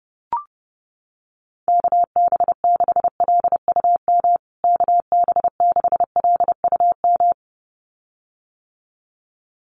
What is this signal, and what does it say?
Morse code, heard as a keyed tone, about 30 wpm, 700 Hz. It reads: KB6LUM KB6LUM
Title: USA Call Signs Repeated 2x 30wpm